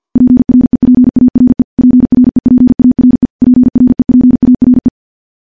Morse code, heard as a keyed tone, about 50 wpm, 250 Hz. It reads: PREPARE PREPARE PREPARE
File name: morse / morse-training-50WPM